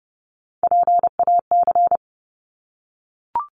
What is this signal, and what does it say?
Morse code, heard as a keyed tone, about 30 wpm, 700 Hz. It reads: PAC
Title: ARRL Sections 30wpm